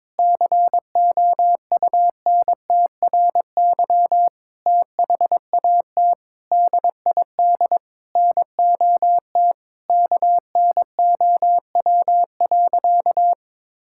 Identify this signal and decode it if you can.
Morse code, heard as a keyed tone, about 22 wpm, 700 Hz. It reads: COUNTRY THAT DID NOT KNOW.
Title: Sentences from Top 1600 Words 22wpm